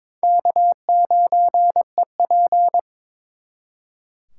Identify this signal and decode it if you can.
Morse code, heard as a keyed tone, about 26 wpm, 700 Hz. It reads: K9EP
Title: Speed Racing - USA 1x2 Call Signs 22wpm